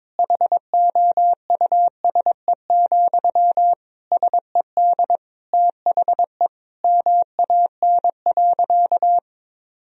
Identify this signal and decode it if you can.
Morse code, heard as a keyed tone, about 22 wpm, 700 Hz. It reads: HOUSE, SED THE MAN.
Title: Sentences from Top 400 Words and Abbreviations 22wpm